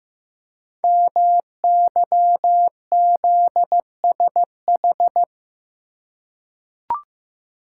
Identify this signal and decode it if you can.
Morse code, heard as a keyed tone, about 15 wpm, 700 Hz. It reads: MYZSH
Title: Code Groups - 5 Letter:Numbers - 15wpm